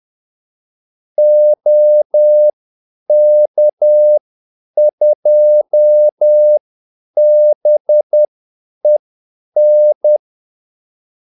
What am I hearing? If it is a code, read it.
Morse code, heard as a keyed tone, about 10 wpm, spaced as for 8 wpm, 600 Hz. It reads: OK2BEN
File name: Callsigns-10-8wpm